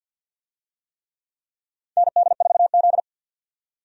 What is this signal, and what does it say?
Morse code, heard as a keyed tone, about 52 wpm, 700 Hz. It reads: ND4Z
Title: USA Call Signs - 4 Characters - 50wpm